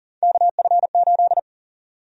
Morse code, heard as a keyed tone, about 40 wpm, 700 Hz. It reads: KF8
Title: USA Call Sign Prefixes - 40wpm